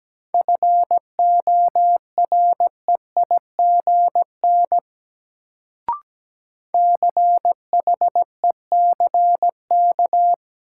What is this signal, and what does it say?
Morse code, heard as a keyed tone, about 17 wpm, 700 Hz. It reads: FOREIGN CHECK
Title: Sentences from Top 1400 Words - New Word Drill Alt - 17wpm